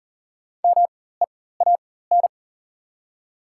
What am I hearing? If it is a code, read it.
Morse code, heard as a keyed tone, about 40 wpm, spaced as for 19 wpm, 700 Hz. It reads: MEAN
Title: 4x Character Spacing - Top 500 Words - 40wpm